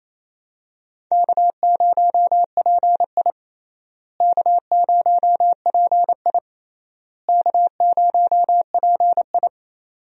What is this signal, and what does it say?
Morse code, heard as a keyed tone, about 28 wpm, 700 Hz. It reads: K0PI K0PI K0PI
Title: USA Call Signs - 4 Characters - Repeated 3x - 28wpm